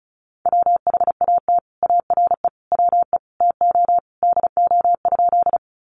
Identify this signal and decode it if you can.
Morse code, heard as a keyed tone, about 34 wpm, 700 Hz. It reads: WHAT ARE WE TO DO?
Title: Sentences from Top 100 Words 35wpm